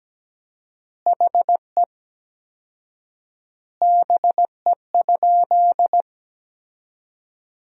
Morse code, heard as a keyed tone, about 17 wpm, 700 Hz. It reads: HE BE?
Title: 4x Word-Spacing - Sentences from Top 200 Words - 17wpm